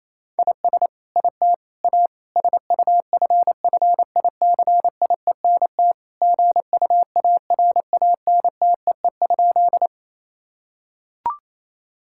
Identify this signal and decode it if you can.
Morse code, heard as a keyed tone, about 28 wpm, 700 Hz. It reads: IS IT A SUFFICIENT GUARANTEE?
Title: Sentences from Top 1600 Words 28wpm